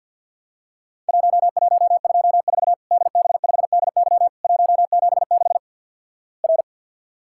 Morse code, heard as a keyed tone, about 48 wpm, 700 Hz. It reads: 1124 DB5DY 176 R
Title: Worked All Europe DX Contest - QTC Exchange with Repeats - 50wpm